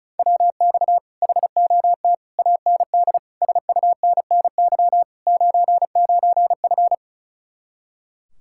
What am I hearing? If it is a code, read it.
Morse code, heard as a keyed tone, about 44 wpm, 700 Hz. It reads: WX HOT AND SUNNY 99F
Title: Speed-Racing Common QSO Phrases - 35wpm